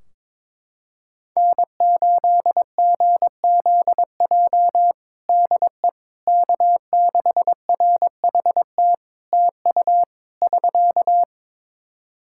Morse code, heard as a keyed tone, about 22 wpm, 700 Hz. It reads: N8GZJ DE K6RHT TU <SK>
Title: Practice QSO - Exchange 4 - Ending - 22wpm